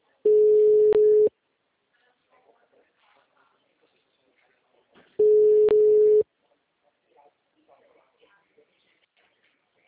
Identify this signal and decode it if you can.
Ringback tone (Europe, 425 Hz), with speech.